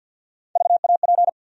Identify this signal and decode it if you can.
Morse code, heard as a keyed tone, about 52 wpm, 700 Hz. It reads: VAP
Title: USA Call Sign Postfixes - 3 Letters 50wpm